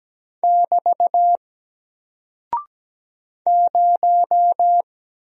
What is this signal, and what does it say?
Morse code, heard as a keyed tone, about 17 wpm, 700 Hz. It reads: <BT> 0
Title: Single Character (Letter:Number:Punctuation:Prosign) Alt 17wpm